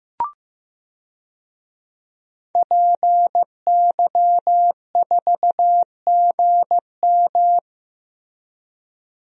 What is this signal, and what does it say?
Morse code, heard as a keyed tone, about 15 wpm, 700 Hz. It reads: PY4GM
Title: DXCC-100 Call Signs - 5 Characters - Repeated 3x - 15wpm